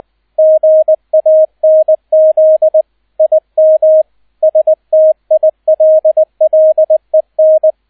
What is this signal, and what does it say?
Morse code, heard as a keyed tone, about 20 wpm, 630 Hz, with radio noise. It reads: GANZ IM STILLEN
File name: 002 Zweiter Streich 20wpm